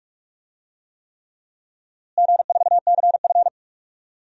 Morse code, heard as a keyed tone, about 46 wpm, 700 Hz. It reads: G4CF
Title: DXCC-100 Call Signs - 4 Characters - Repeated 3x - 45wpm